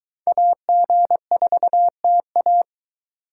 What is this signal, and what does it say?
Morse code, heard as a keyed tone, about 23 wpm, 700 Hz. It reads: AG4TA